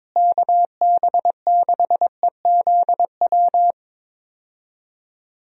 Morse code, heard as a keyed tone, about 22 wpm, 700 Hz. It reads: KB6EZW
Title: USA Call Signs - 6 Characters - Repeated 2x - 22wpm